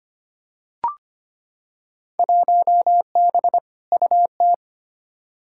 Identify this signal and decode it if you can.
Morse code, heard as a keyed tone, about 25 wpm, 700 Hz. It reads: 1B UT
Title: Field Day Exchanges - 25wpm